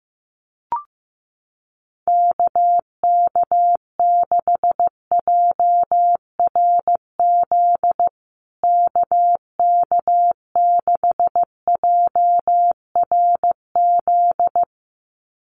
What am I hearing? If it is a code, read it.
Morse code, heard as a keyed tone, about 15 wpm, 700 Hz. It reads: KK6JRZ KK6JRZ
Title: USA Call Signs Repeated 2x 15wpm